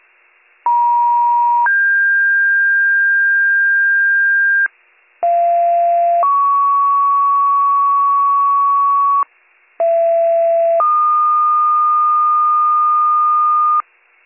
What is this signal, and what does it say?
Two-tone page, heard as a one-second tone, then a three-second tone, about 950 then 1600 Hz, about 690 then 1120 Hz, about 670 then 1220 Hz. It is heard from 0.7 to 13.8 s.